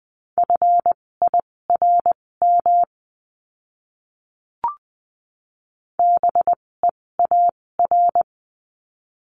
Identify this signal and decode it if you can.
Morse code, heard as a keyed tone, about 20 wpm, spaced as for 16 wpm, 700 Hz. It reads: FIRM BEAR
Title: 256 Lesson 42a - Words 12-20wpm